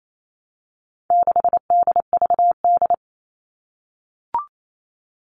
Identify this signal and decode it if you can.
Morse code, heard as a keyed tone, about 28 wpm, 700 Hz. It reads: 6DVD
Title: All Cut Numbers - 4-Digit 28wpm